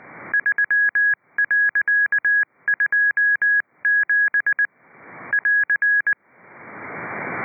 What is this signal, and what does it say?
Morse code, heard as a keyed tone, about 20 wpm, 1650 Hz. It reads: 3.27 <AR>